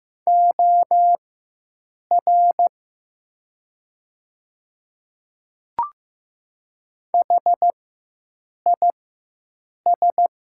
Morse code, heard as a keyed tone, about 15 wpm, spaced as for 7 wpm, 700 Hz. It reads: OR HIS
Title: 4x Character Spacing - Top 100 Words - 15wpm